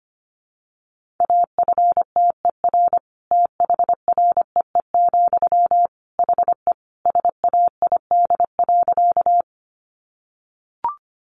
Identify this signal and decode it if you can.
Morse code, heard as a keyed tone, about 25 wpm, 700 Hz. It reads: AFTER THREE, HE SAID.
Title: Sentences from Top 200 Words 25wpm